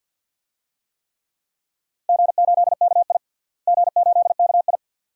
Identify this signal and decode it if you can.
Morse code, heard as a keyed tone, about 47 wpm, 700 Hz. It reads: G8KI G8KI
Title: DXCC-100 Call Signs - 4 Characters - Repeated 2x - 50wpm